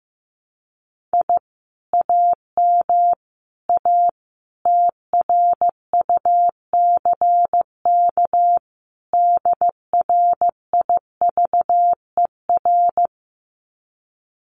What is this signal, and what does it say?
Morse code, heard as a keyed tone, about 15 wpm, 700 Hz. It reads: I AM A TRUCK DRIVER
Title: Common QSO Phrases Encore 15wpm